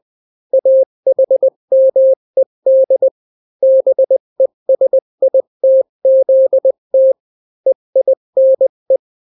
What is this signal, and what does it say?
Morse code, heard as a keyed tone, about 20 wpm, 530 Hz. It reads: AHMED BESITZT EINE
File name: daily / 18wpm